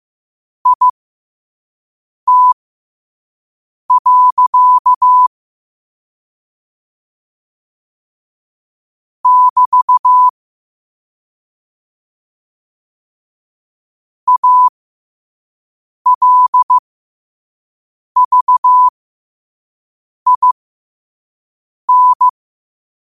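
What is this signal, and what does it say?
Morse code, heard as a keyed tone, about 15 wpm, spaced as for 5 wpm, 1000 Hz. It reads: IT. <BT> ALVIN
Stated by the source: Morse practice at 5 wpm.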